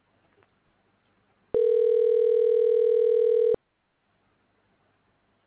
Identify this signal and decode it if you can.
Ringback tone (North America), 440 + 480 Hz.